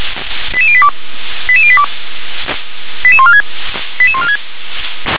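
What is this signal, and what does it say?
Five-tone selective call (ZVEI), heard as quick runs of five tones, tones of about 70 ms. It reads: 88082 88082 88146 88146
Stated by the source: ZVEI five-tone selective call received on SDR.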